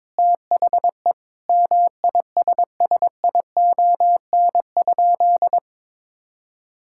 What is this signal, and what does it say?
Morse code, heard as a keyed tone, about 22 wpm, 700 Hz. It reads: THE MISSION?
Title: Sentences from Top 1600 Words 22wpm